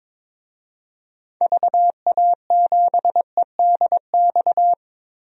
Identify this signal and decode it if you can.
Morse code, heard as a keyed tone, about 22 wpm, 700 Hz. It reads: VA7EDX
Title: DXCC-100 Call Signs - 6 Characters - 22wpm